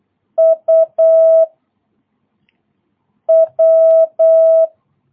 Morse code, heard as a keyed tone, about 8 wpm, 650 Hz. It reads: U W
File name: jh8cbh / 2022cw